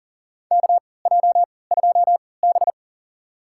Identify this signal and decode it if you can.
Morse code, heard as a keyed tone, about 40 wpm, spaced as for 23 wpm, 700 Hz. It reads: KJ2B
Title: USA Call Signs - 3x Character Spacing - 40wpm